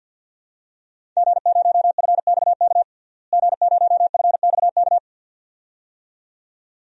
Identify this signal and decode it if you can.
Morse code, heard as a keyed tone, about 51 wpm, 700 Hz. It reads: G0FXK G0FXK
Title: DXCC-100 Call Signs - 5 Characters - Repeated 2x - 50wpm